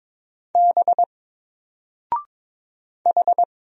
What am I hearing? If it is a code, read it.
Morse code, heard as a keyed tone, about 22 wpm, 700 Hz. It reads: B H
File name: Single-Letter-Number-22wpm